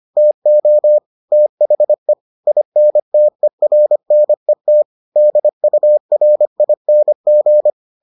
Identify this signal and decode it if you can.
Morse code, heard as a keyed tone, about 25 wpm, 600 Hz. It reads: TO THE INTERNET DURING